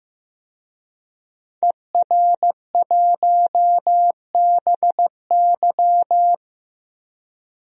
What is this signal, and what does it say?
Morse code, heard as a keyed tone, about 15 wpm, 700 Hz. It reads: ER1BY
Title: DXCC-100 Call Signs - 5 Characters - Repeated 2x - 15wpm